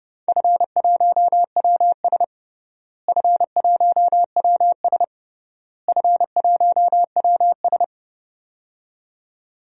Morse code, heard as a keyed tone, about 30 wpm, 700 Hz. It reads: F1WS F1WS F1WS
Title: DXCC-100 Call Signs - 4 Characters - Repeated 3x - 30wpm